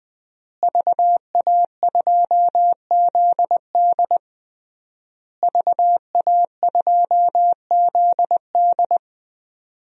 Morse code, heard as a keyed tone, about 20 wpm, 700 Hz. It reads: VA2ZD VA2ZD
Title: DXCC-100 Call Signs - 5 Characters - Repeated 3x - 20wpm